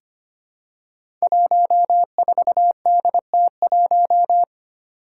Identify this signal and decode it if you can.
Morse code, heard as a keyed tone, about 25 wpm, 700 Hz. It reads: 14DT1
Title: Five Challenging Letter-Number Combinations 25wpm